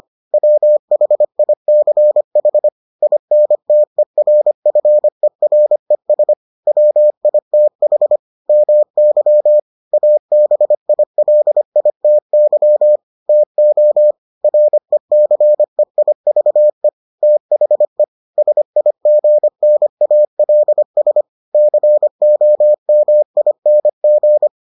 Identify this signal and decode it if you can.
Morse code, heard as a keyed tone, about 25 wpm, 600 Hz. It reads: WHICH INTERFERES WITH MY ABILITY TO RECEIVE THE SIGNALS COMING